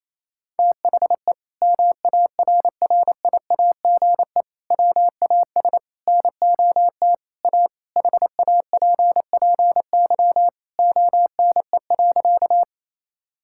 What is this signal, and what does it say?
Morse code, heard as a keyed tone, about 28 wpm, 700 Hz. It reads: THE MARRIAGE WAS NOT A HAPPY ONE.
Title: Sentences from Top 1000 Words - Review Part 5 - 28wpm